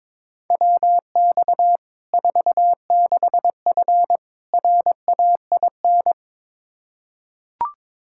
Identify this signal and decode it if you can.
Morse code, heard as a keyed tone, about 21 wpm, 700 Hz. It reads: WX 46F RAIN
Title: Practice QSO - Exchange 2 - Info - 22wpm